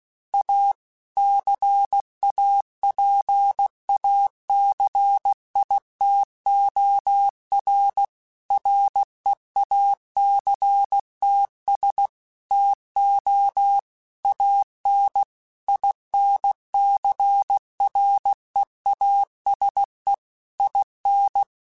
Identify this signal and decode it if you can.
Morse code, heard as a keyed tone, about 16 wpm, 800 Hz. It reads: A CAPACITOR REACTS TO AN INCREASE IN